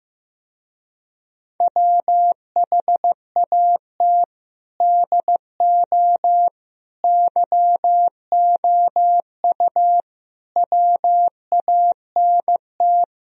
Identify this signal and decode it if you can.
Morse code, heard as a keyed tone, about 15 wpm, 700 Hz. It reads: WHAT DO YOU WANT
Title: Speed-Racing - Sets of 5 Words 15wpm